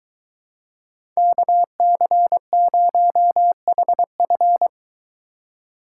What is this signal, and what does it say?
Morse code, heard as a keyed tone, about 23 wpm, 700 Hz. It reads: KC0HF